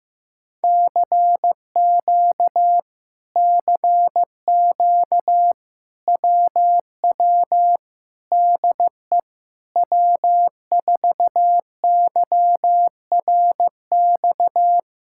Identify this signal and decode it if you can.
Morse code, heard as a keyed tone, about 15 wpm, 700 Hz. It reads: CQ CQ WW DE W4YRX
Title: Speed-Racing Common QSO Phrases Encore - 15wpm